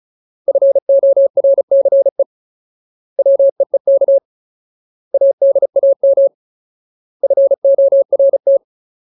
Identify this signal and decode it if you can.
Morse code, heard as a keyed tone, about 34 wpm, 550 Hz. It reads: FORCE WEEK ADAM FORT
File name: Ultimate Word List / 35WPM Ultimate